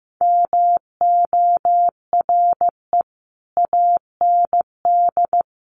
Morse code, heard as a keyed tone, about 14 wpm, 700 Hz. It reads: MORE AND
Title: Sets of 3 Words 15wpm